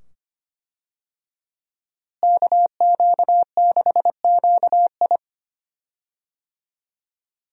Morse code, heard as a keyed tone, about 25 wpm, 700 Hz. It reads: KQ6QI
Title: USA Call Signs - 5 Characters - Repeated 2x - 25wpm